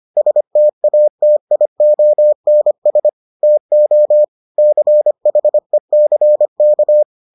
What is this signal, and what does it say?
Morse code, heard as a keyed tone, about 25 wpm, 600 Hz. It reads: STATIONS TO CHECK